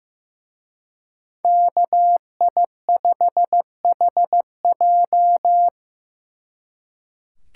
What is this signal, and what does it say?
Morse code, heard as a keyed tone, about 18 wpm, 700 Hz. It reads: KI5HJ
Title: Speed Racing - USA 2x2 Call Signs 15wpm